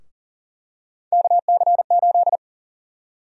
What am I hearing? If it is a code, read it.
Morse code, heard as a keyed tone, about 39 wpm, 700 Hz. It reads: KC8
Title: USA Call Sign Prefixes - Repeated 2x - 40wpm